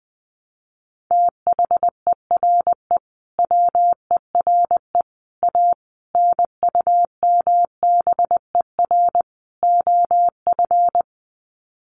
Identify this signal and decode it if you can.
Morse code, heard as a keyed tone, about 20 wpm, 700 Hz. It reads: THERE WERE A NUMBER OF
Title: Sets of 5 Words Encore2 20wpm